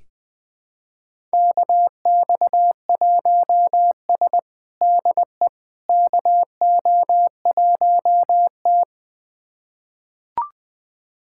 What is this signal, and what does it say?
Morse code, heard as a keyed tone, about 20 wpm, 700 Hz. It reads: KX1S DE KO1T
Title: USA Callsign DE Callsign - 4 Characters - 20wpm